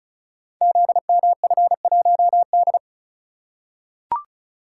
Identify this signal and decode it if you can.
Morse code, heard as a keyed tone, about 36 wpm, 700 Hz. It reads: ZMF1D
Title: Code Groups - 5 Letter:Numbers - 35wpm